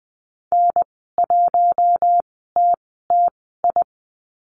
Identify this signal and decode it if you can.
Morse code, heard as a keyed tone, about 20 wpm, spaced as for 14 wpm, 700 Hz. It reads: N1TTI
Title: 048 Lesson 9g - Number 4 - Call Signs 20wpm